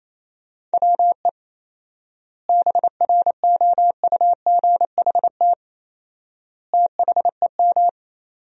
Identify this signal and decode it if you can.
Morse code, heard as a keyed tone, about 28 wpm, 700 Hz. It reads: WE BROUGHT THEM
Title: 4x Word-Spacing - Sentences from Top 500 Words - 28wpm